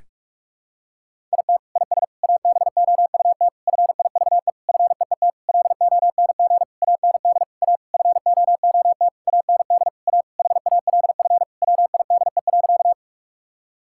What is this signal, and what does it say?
Morse code, heard as a keyed tone, about 46 wpm, 700 Hz. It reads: IT IS ABOUT FIVE FEET LONG AND A FOOT AND A HALF WIDE.
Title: Sentences from Top 1000 Words - Review Part 6 - 45wpm